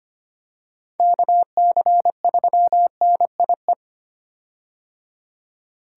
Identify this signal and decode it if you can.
Morse code, heard as a keyed tone, about 25 wpm, 700 Hz. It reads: KC3NIE
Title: K1USN SST QSO Practice - 25wpm